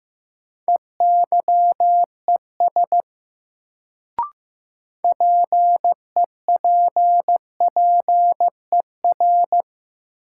Morse code, heard as a keyed tone, about 15 wpm, 700 Hz. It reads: EYES PEPPER
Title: Sets of 5 Words - New Word Drill - 15wpm